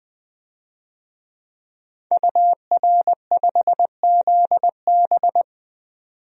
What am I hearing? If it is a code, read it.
Morse code, heard as a keyed tone, about 21 wpm, 700 Hz. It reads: UR5ZB
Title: DXCC-100 Call Signs - 5 Characters - 20wpm